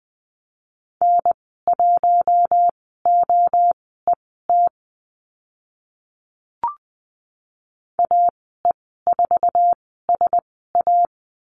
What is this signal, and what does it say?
Morse code, heard as a keyed tone, about 20 wpm, spaced as for 14 wpm, 700 Hz. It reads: N1OET AE4SA
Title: 048 Lesson 9g - Number 4 - Call Signs 20wpm